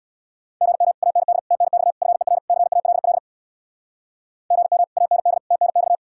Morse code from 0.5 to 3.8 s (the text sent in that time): GM AND TNX FER CALL